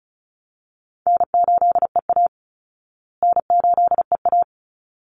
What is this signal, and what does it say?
Morse code, heard as a keyed tone, about 34 wpm, 700 Hz. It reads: N8EA N8EA
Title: Call Signs Repeated 3x - 35wpm